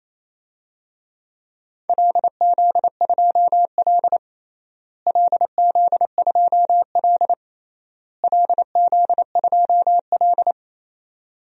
Morse code, heard as a keyed tone, about 28 wpm, 700 Hz. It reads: LZ2L LZ2L LZ2L
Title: DXCC-100 Call Signs - 4 Characters - Repeated 3x - 28wpm